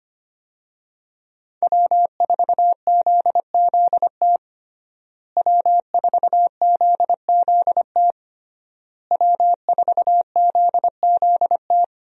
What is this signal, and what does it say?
Morse code, heard as a keyed tone, about 25 wpm, 700 Hz. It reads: W4ZZT W4ZZT W4ZZT
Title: USA Call Signs - 5 Characters - Repeated 3x - 25wpm